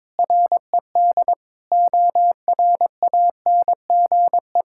Morse code, heard as a keyed tone, about 22 wpm, 700 Hz. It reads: RED ORANGE
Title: Pangram Challenge v2 - 22wpm